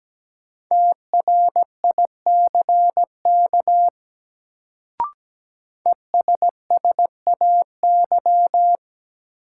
Morse code, heard as a keyed tone, about 17 wpm, 700 Hz. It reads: TRICK ESSAY
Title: Sentences from Top 1500 Words - New Word Drill Alt - 17wpm